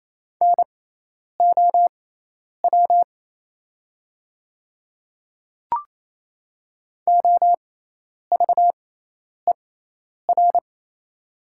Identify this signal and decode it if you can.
Morse code, heard as a keyed tone, about 28 wpm, spaced as for 10 wpm, 700 Hz. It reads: NOW OVER
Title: 6x Character Spacing - Top 100 Words - 28wpm